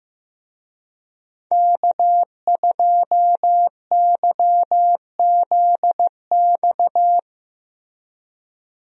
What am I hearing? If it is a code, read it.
Morse code, heard as a keyed tone, about 15 wpm, 700 Hz. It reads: K2YZX
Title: USA Call Signs - 5 Characters - Repeated 3x - 15wpm